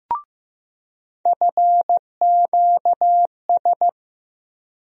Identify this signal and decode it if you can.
Morse code, heard as a keyed tone, about 15 wpm, 700 Hz. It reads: FQS
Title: USA Call Sign Postfixes - 3 Letters 15wpm